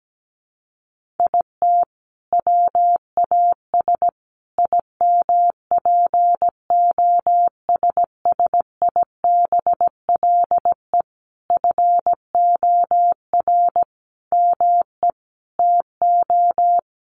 Morse code, heard as a keyed tone, about 17 wpm, 700 Hz. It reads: IT WAS IMPOSSIBLE FOR ME TO
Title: Sentences from Top 1100 Words 17wpm